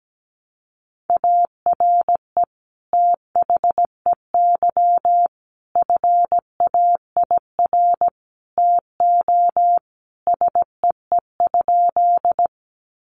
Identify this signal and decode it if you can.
Morse code, heard as a keyed tone, about 17 wpm, 700 Hz. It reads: ARE THEY FAIR TO SEE?
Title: Sentences from Top 1300 Words 17wpm